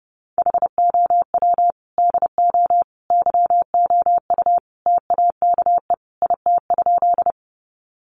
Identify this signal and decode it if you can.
Morse code, heard as a keyed tone, about 30 wpm, 700 Hz. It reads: HOW DO YOU TAKE IT?
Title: Sentences from Top 200 Words 30wpm